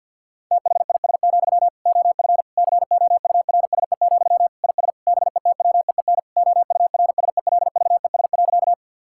Morse code, heard as a keyed tone, about 50 wpm, 700 Hz. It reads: THIS, OF COURSE, IS BETWEEN OURSELVES.
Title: Sentences from Top 1600 Words 50wpm